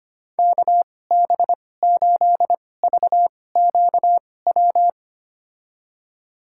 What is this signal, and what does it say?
Morse code, heard as a keyed tone, about 25 wpm, spaced as for 18 wpm, 700 Hz. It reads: KB8VQW
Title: USA Call Signs - 2x Character Spacing - 25wpm